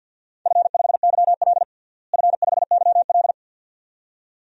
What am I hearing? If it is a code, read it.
Morse code, heard as a keyed tone, about 51 wpm, 700 Hz. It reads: F5YL F5YL
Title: DXCC-100 Call Signs - 4 Characters - Repeated 2x - 50wpm